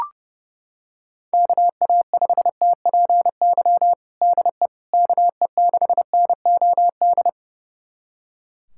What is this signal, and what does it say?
Morse code, heard as a keyed tone, about 30 wpm, 700 Hz. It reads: KA5TPY DE KE6NOD
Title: USA Callsign DE Callsign - 6 Characters - 30wpm